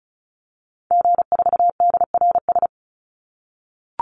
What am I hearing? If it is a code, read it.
Morse code, heard as a keyed tone, about 34 wpm, 700 Hz. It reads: G4DRS
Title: Call Signs - 35wpm